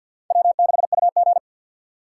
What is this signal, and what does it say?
Morse code, heard as a keyed tone, about 52 wpm, 700 Hz. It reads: W6UG
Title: USA Call Signs - 6x Character Spacing - 50wpm